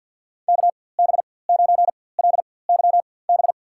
Morse code, heard as a keyed tone, about 52 wpm, spaced as for 24 wpm, 700 Hz. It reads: KB9LYB
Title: USA Call Signs - 6 Characters - 4x Character Spacing - 50wpm